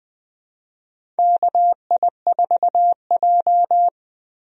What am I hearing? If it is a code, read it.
Morse code, heard as a keyed tone, about 21 wpm, 700 Hz. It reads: KI4J
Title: USA Call Signs - 4 Characters - 20wpm